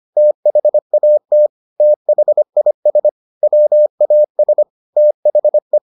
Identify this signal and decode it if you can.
Morse code, heard as a keyed tone, about 25 wpm, 600 Hz. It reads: THAT THIS WAS THE